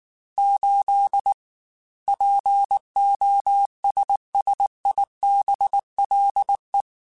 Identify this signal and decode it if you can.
Morse code, heard as a keyed tone, about 19 wpm, 800 Hz. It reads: 8 POSSIBLE
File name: Technical Subjects / Technical Subjects-mp3s / AND Gate @18